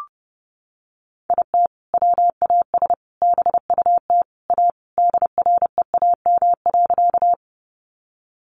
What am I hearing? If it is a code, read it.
Morse code, heard as a keyed tone, about 30 wpm, 700 Hz. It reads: IT WAS BUT A DREAM.
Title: Sentences from Top 1500 Words 30wpm